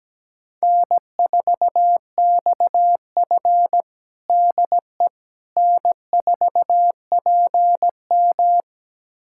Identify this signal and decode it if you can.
Morse code, heard as a keyed tone, about 17 wpm, 700 Hz. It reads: N4XF DE N4PM
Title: USA Callsign DE Callsign - 4 Characters - 17wpm